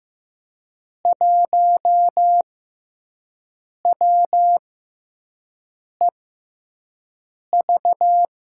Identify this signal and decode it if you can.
Morse code, heard as a keyed tone, about 15 wpm, spaced as for 5 wpm, 700 Hz. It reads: 1WEV
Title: USA Call Signs - 5 Characters - 6x Character Spacing - 15wpm